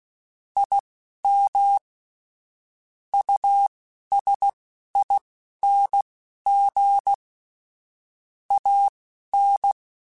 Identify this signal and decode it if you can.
Morse code, heard as a keyed tone, about 16 wpm, spaced as for 11 wpm, 800 Hz. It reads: IM USING AN
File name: (spaced) / QSOs Spaced-mp3s / WD0FBT 1 @15 sp